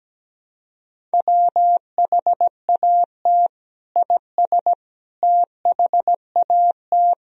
Morse code, heard as a keyed tone, about 17 wpm, 700 Hz. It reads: WHAT IS THAT
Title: Sentences from Top 1000 Words - Review Part 1 - 17wpm